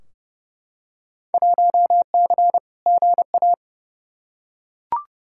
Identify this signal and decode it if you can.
Morse code, heard as a keyed tone, about 31 wpm, 700 Hz. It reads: 1C GA
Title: Field Day Exchanges - 30wpm